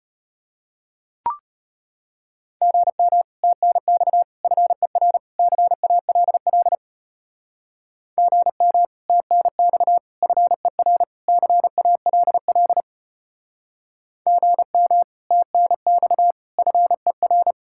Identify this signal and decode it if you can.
Morse code, heard as a keyed tone, about 32 wpm, 700 Hz. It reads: GM TNX FER CALL GM TNX FER CALL GM TNX FER